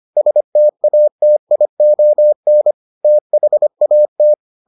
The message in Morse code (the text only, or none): STATION THAT